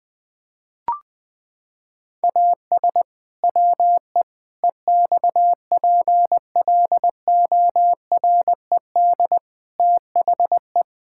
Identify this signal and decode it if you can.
Morse code, heard as a keyed tone, about 20 wpm, 700 Hz. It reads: AS WE EXPLORED THE
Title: Pangram Challenge v2 - 20wpm